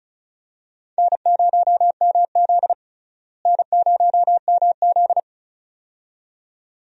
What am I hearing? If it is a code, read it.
Morse code, heard as a keyed tone, about 36 wpm, 700 Hz. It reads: N0MZ N0MZ
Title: USA Call Signs - 4 Characters - Repeated 2x - 35wpm